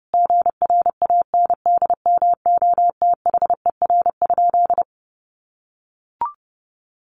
Morse code, heard as a keyed tone, about 30 wpm, 700 Hz. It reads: GRANDMOTHER?
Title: Sentences from Top 1200 Words 30wpm